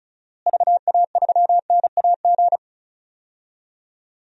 Morse code, heard as a keyed tone, about 34 wpm, 700 Hz. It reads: VA3NAG
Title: DXCC-100 Call Signs - 6 Characters - Repeated 3x - 35wpm